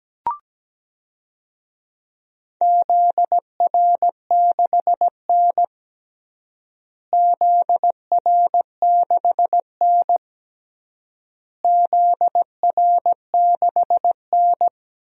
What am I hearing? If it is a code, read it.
Morse code, heard as a keyed tone, about 17 wpm, 700 Hz. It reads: ZR6N ZR6N ZR6N
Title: DXCC-100 Call Signs - 4 Characters - Repeated 3x - 17wpm